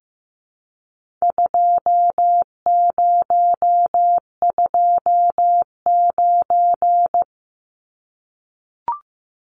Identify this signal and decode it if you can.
Morse code, heard as a keyed tone, about 15 wpm, 700 Hz. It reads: 2029
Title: Numbers - 4-Digit 15wpm